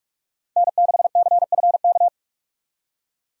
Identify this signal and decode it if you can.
Morse code, heard as a keyed tone, about 42 wpm, 700 Hz. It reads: N6CFK
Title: USA Call Signs - 5 Characters - Repeated 3x - 45wpm